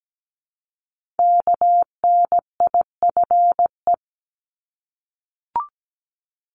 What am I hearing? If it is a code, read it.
Morse code, heard as a keyed tone, about 17 wpm, 700 Hz. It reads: KNIFE
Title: Multi-letter Phonemes Repeated 3x 17wpm